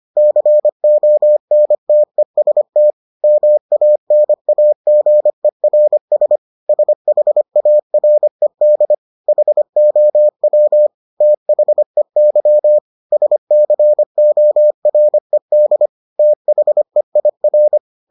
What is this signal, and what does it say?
Morse code, heard as a keyed tone, about 25 wpm, 600 Hz. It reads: CONTEST MANAGERS SHARED HOW THEY SCORED THEIR